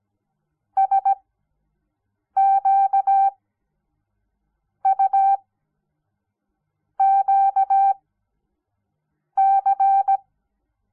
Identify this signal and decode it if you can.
Morse code, heard as a keyed tone, about 17 wpm, spaced as for 5 wpm, 780 Hz. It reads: SQUQC
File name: Onlinelehrgang / morsen / DJ4UF Morsen06